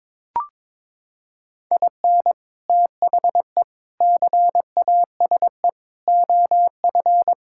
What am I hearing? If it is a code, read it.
Morse code, heard as a keyed tone, about 22 wpm, 700 Hz. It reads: IN THE CASE OF
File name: practice / 22wpm / Sentences with 3-Word Phrases - 22wpm